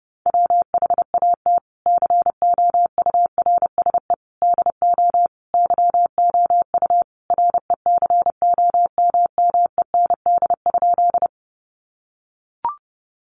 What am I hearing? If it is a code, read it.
Morse code, heard as a keyed tone, about 30 wpm, 700 Hz. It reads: WHAT COURSE DO YOU RECOMMEND?
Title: Sentences from Top 1200 Words 30wpm